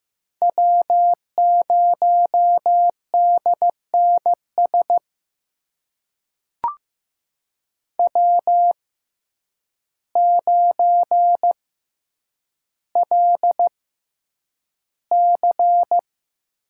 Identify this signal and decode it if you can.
Morse code, heard as a keyed tone, about 15 wpm, spaced as for 8 wpm, 700 Hz. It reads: W0DNS W9LC
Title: USA Call Signs - 5 Characters - 6x Character Spacing - 15wpm